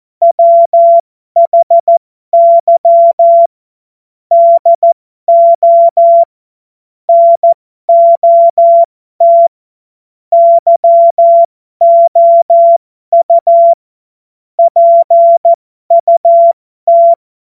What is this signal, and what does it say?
Morse code, heard as a keyed tone, about 14 wpm, spaced as for 12 wpm, 680 Hz. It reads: WHY DO NOT YOU PUT
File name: Morsekurs / Witze